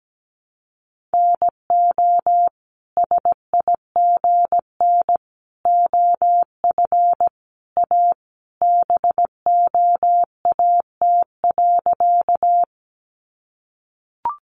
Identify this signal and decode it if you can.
Morse code, heard as a keyed tone, about 17 wpm, 700 Hz. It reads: NO SIGN OF A BOAT.
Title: Sentences from Top 1000 Words 17wpm